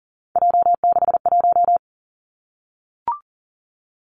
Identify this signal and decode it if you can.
Morse code, heard as a keyed tone, about 40 wpm, 700 Hz. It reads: J61
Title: Three Challenging Letter-Number Combinations 40wpm